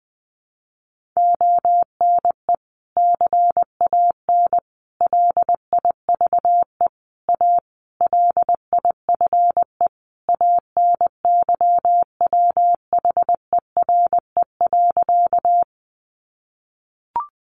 Morse code, heard as a keyed tone, about 19 wpm, 700 Hz. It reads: ONE CAN LIVE A LIFE ANYWHERE.
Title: Sentences from Top 1400 Words 20wpm